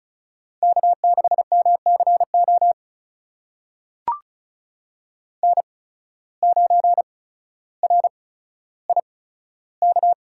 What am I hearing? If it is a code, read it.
Morse code, heard as a keyed tone, about 36 wpm, spaced as for 15 wpm, 700 Hz. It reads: K6MCO N9RIK
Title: USA Call Signs - 5 Characters - 8x Character Spacing - 35wpm